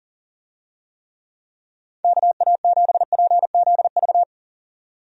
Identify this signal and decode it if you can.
Morse code, heard as a keyed tone, about 40 wpm, 700 Hz. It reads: KA7PZV
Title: USA Call Signs - 6 Characters - 40wpm